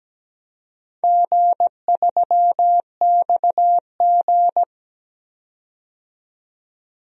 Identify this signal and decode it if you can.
Morse code, heard as a keyed tone, about 17 wpm, 700 Hz. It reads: G3XG
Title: DXCC-100 Call Signs - 4 Characters - 17wpm